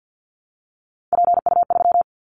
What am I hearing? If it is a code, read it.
Morse code, heard as a keyed tone, about 50 wpm, 700 Hz. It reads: PU3
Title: Callsign Prefixes for Top 50 DXCC - 50wpm